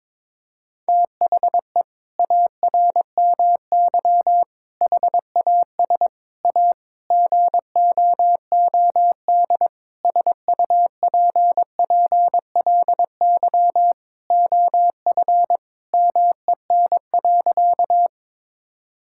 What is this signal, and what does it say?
Morse code, heard as a keyed tone, about 21 wpm, 700 Hz. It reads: THE ARMY HAS A GOOD SUPPLY OF MEN.
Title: Sentences from Top 1000 Words - Review Part 5 - 22wpm